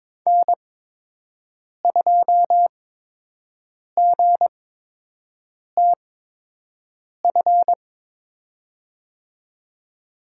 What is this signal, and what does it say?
Morse code, heard as a keyed tone, about 22 wpm, spaced as for 6 wpm, 700 Hz. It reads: N2GTF
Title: USA Call Signs - 8x Character Spacing - 22wpm